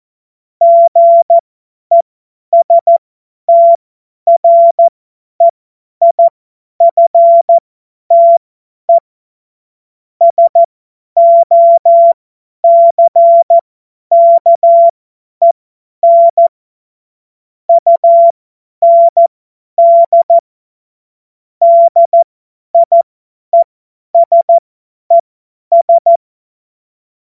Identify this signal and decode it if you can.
Morse code, heard as a keyed tone, about 14 wpm, spaced as for 10 wpm, 680 Hz. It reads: GESTREIFTE SOCKEN UND DIESES
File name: Morsekurs / Witze / Witz 03. Januar 2023 50 F70